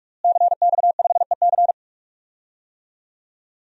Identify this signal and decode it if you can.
Morse code, heard as a keyed tone, about 46 wpm, 700 Hz. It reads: CX5EC
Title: DXCC-100 Call Signs - 5 Characters - Repeated 3x - 45wpm